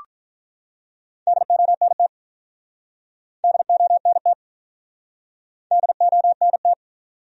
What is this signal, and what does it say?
Morse code, heard as a keyed tone, about 47 wpm, 700 Hz. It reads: DONT DONT DONT